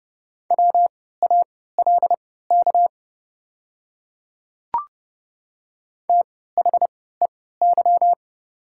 Morse code, heard as a keyed tone, about 30 wpm, spaced as for 17 wpm, 690 Hz. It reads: WALK THEY
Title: 3x Character Spacing - Top 300 Words - 30wpm